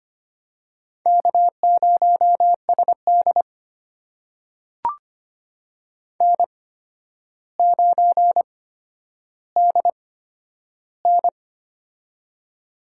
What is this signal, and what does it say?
Morse code, heard as a keyed tone, about 25 wpm, spaced as for 11 wpm, 700 Hz. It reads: K0SD N9DN
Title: USA Call Signs - 4 Characters - 8x Character Spacing - 25wpm